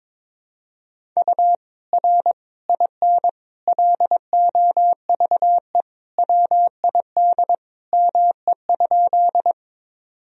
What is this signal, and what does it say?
Morse code, heard as a keyed tone, about 22 wpm, 700 Hz. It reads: U R IN LOVE WID ME?
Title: Sentences from Top 400 Words and Abbreviations 22wpm